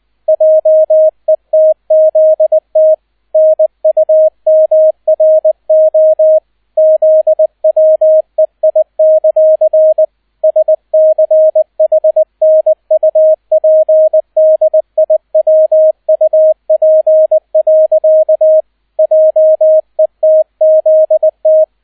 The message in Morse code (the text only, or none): JETZT NUMRO ZWEI; SCHNUPDIWUP. JETZT